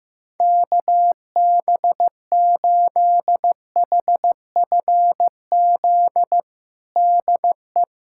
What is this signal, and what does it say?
Morse code, heard as a keyed tone, about 15 wpm, 700 Hz. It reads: KB8HFZ DE
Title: USA Callsign DE Callsign - 6 Characters - 15wpm